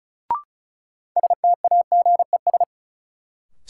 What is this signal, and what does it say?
Morse code, heard as a keyed tone, about 36 wpm, 700 Hz. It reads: STAGES
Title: Sets of 5 Words Encore - New Word Drill - 35wpm